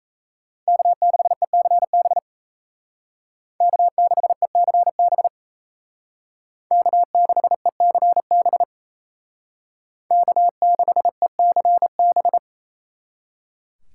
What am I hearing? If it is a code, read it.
Morse code, heard as a keyed tone, about 36 wpm, 700 Hz. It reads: K6ECB K6ECB K6ECB K6ECB